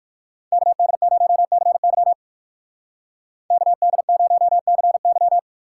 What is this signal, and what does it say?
Morse code, heard as a keyed tone, about 59 wpm, 700 Hz. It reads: KD0CY KD0CY